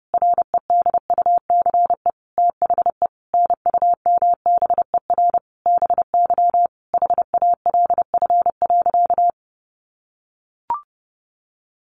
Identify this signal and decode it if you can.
Morse code, heard as a keyed tone, about 30 wpm, 700 Hz. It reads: REDUCE THE NUMBER BY HALF.
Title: Sentences from Top 1100 Words 30wpm